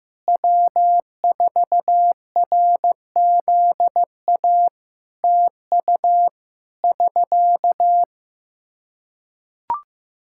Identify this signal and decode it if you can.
Morse code, heard as a keyed tone, about 15 wpm, 700 Hz. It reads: W4RZA TU <SK>
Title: Practice QSO - Exchange 4 - Ending - 15wpm